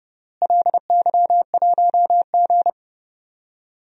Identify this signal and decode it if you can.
Morse code, heard as a keyed tone, about 30 wpm, 700 Hz. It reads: LY1G
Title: DXCC-100 Call Signs - 4 Characters - Repeated 3x - 30wpm